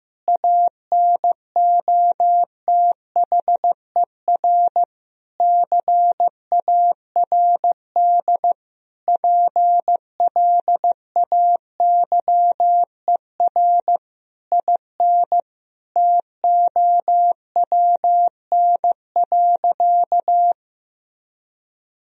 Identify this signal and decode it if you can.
Morse code, heard as a keyed tone, about 16 wpm, 700 Hz. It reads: ANOTHER CARD PLAYER IN TOWN.
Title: Sentences from Top 1000 Words - Review Part 4 - 15wpm